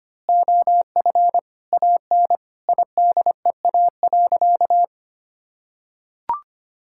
Morse code, heard as a keyed tone, about 25 wpm, 700 Hz. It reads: OF AN IDEA.
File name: practice / 25wpm / Sentences with 2-Word Phrases - 25wpm